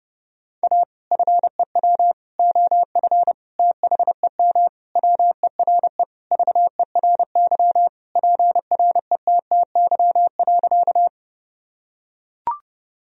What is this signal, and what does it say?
Morse code, heard as a keyed tone, about 32 wpm, 700 Hz. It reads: A FEW OF THEM WERE VERY PRETTY.
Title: Sentences from Top 1000 Words - Review Part 1 - 30wpm